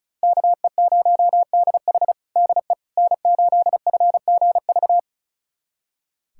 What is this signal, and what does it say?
Morse code, heard as a keyed tone, about 36 wpm, 700 Hz. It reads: KE0DH DE N8FGV
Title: USA Callsign DE Callsign - 5 Characters - 35wpm